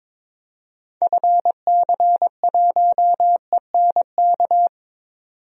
Morse code, heard as a keyed tone, about 22 wpm, 700 Hz. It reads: FC1ENK
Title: DXCC-100 Call Signs - 6 Characters - 22wpm